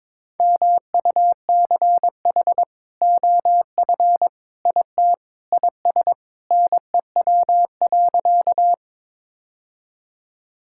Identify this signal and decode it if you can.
Morse code, heard as a keyed tone, about 22 wpm, 700 Hz. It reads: MUCH OF IT IS NEW.
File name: practice / 22wpm / Sentences with 2-Word Phrases - 22wpm